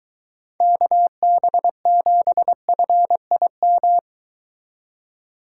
Morse code, heard as a keyed tone, about 22 wpm, 700 Hz. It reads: KB7FIM